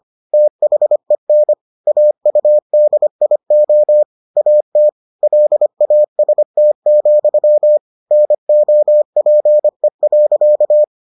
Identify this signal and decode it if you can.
Morse code, heard as a keyed tone, about 25 wpm, 600 Hz. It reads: THEN AUDIO AT LAST, NOPE.